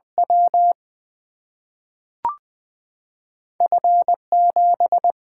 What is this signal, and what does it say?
Morse code, heard as a keyed tone, about 20 wpm, 700 Hz. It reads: W F 7
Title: Single-Letter Single-Number 20wpm